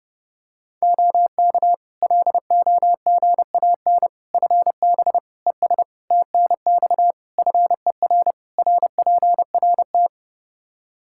Morse code, heard as a keyed tone, about 30 wpm, 700 Hz. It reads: OK LOGAN FB ES TNX FER RPRT
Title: Practice QSO - Exchange 2 - Info - 30wpm